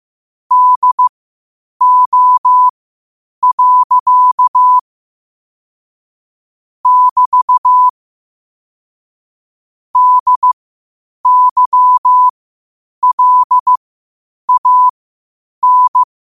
Morse code, heard as a keyed tone, about 15 wpm, spaced as for 8 wpm, 1000 Hz. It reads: DO. <BT> DYLAN
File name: qotdfiles / 8wpm